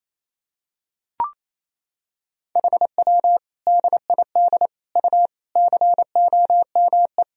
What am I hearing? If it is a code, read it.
Morse code, heard as a keyed tone, about 28 wpm, 700 Hz. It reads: HW DID U COME
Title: Sentences from Top 100 Words and Abbreviations 28wpm